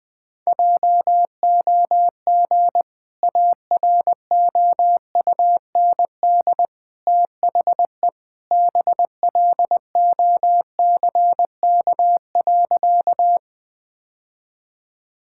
Morse code, heard as a keyed tone, about 20 wpm, 700 Hz. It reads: JOG AROUND THE BLOCK.
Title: Easy Sentences with Hints - 20wpm